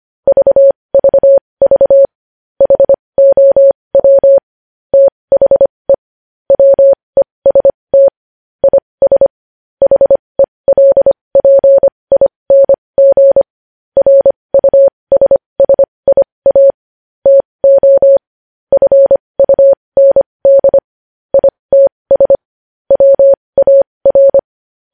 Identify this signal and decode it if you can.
Morse code, heard as a keyed tone, about 25 wpm, spaced as for 20 wpm, 550 Hz. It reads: VVV HOW THE WEST IS HELPING RUSSIA TO FUND ITS WAR
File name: podcast / News headlines-1x-25x20